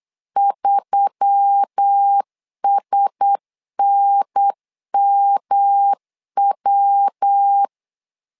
Morse code, heard as a keyed tone, about 8 wpm, 800 Hz. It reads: 3SNMW
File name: LEZIONI / 4 Q1S3 8wpm 3037